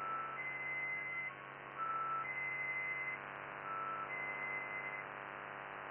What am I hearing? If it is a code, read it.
Emergency Alert System (SAME) tones (header bursts), heard as harsh buzzing bursts. It is heard from 0.3 to 5.1 s.